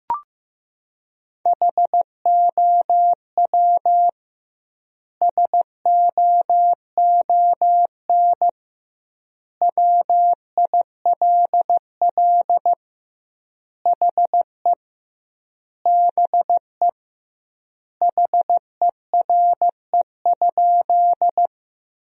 Morse code, heard as a keyed tone, about 15 wpm, 700 Hz. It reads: HOW SOON WILL HE BE HERE?
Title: 2x Word-Spacing - Sentences from Top 400 Words - 15wpm